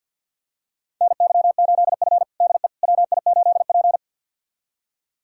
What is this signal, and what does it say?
Morse code, heard as a keyed tone, about 46 wpm, 700 Hz. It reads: NY8F DE WI8P
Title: USA Callsign DE Callsign - 4 Characters - 50wpm